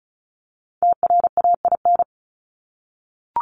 Morse code, heard as a keyed tone, about 34 wpm, 700 Hz. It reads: TRAIN
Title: ING Suffix - Top 100 Words with Base Words 35wpm